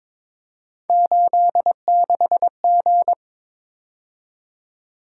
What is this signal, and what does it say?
Morse code, heard as a keyed tone, about 22 wpm, 700 Hz. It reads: 86G
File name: practice / 22wpm / Three Letter-Number Combinations - 22wpm